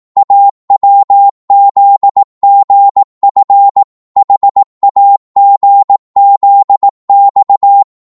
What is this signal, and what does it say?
Morse code, heard as a keyed tone, about 18 wpm, 800 Hz. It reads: AWZGF HAGZX
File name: Morse / ltr-18wpm